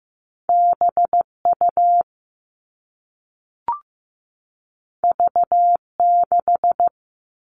Morse code, heard as a keyed tone, about 15 wpm, 700 Hz. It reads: BU V6
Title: All Cut Numbers - 2-Digit 15wpm